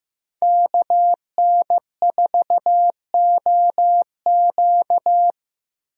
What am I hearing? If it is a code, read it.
Morse code, heard as a keyed tone, about 15 wpm, 700 Hz. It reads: KN4OQ
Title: USA Call Signs - 5 Characters - 15wpm